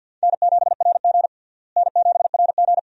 Morse code, heard as a keyed tone, about 48 wpm, 700 Hz. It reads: N7RG N7RG
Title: USA Call Signs - 4 Characters - Repeated 2x - 50wpm